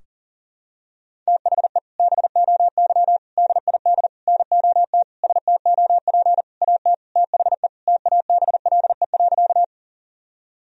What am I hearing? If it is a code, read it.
Morse code, heard as a keyed tone, about 40 wpm, 700 Hz. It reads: THE BOY DID NOT STOP AT THE TABLE.
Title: Sentences from Top 1000 Words - Review Part 2 - 40wpm